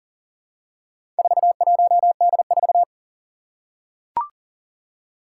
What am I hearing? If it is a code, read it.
Morse code, heard as a keyed tone, about 40 wpm, 700 Hz. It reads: 41D4
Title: Four Challenging Letter-Number Combinations 40wpm